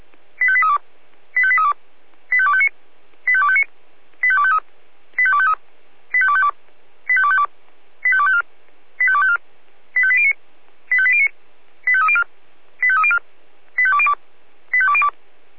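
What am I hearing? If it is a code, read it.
Five-tone selective call (ZVEI), heard as quick runs of five tones, tones of about 70 ms. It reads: 85832 85832 85259 85259 85253 85253 85252 85252 85254 85254 85809 85809 85304 85304 85202 85202
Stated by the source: ZVEI five-tone call received by SDR.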